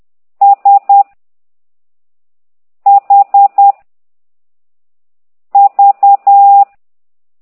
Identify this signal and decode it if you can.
Morse code, heard as a keyed tone, about 10 wpm, 800 Hz. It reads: S H V